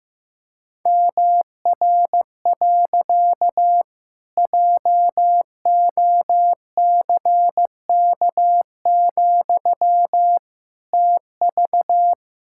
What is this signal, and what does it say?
Morse code, heard as a keyed tone, about 15 wpm, 700 Hz. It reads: MR. JOCK, TV
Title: Pangram Challenge v2 - 15wpm